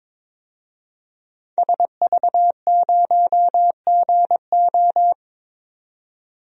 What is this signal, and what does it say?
Morse code, heard as a keyed tone, about 22 wpm, 700 Hz. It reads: SV0GO
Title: DXCC-100 Call Signs - 5 Characters - 22wpm